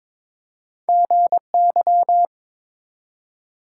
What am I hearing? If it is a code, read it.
Morse code, heard as a keyed tone, about 23 wpm, 700 Hz. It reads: GY
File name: practice / 22wpm / Two Letter-Number Combinations - 22wpm